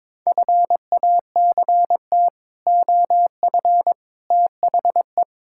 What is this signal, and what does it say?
Morse code, heard as a keyed tone, about 22 wpm, 700 Hz. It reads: FACT OF THE
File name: practice / 22wpm / Build A Phrase - 5-Word Phrases - 22wpm